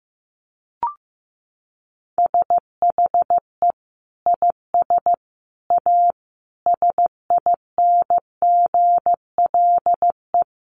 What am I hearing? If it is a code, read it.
Morse code, heard as a keyed tone, about 15 wpm, 700 Hz. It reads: SHE IS A SINGLE
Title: Sentences from Top 700 Words 15wpm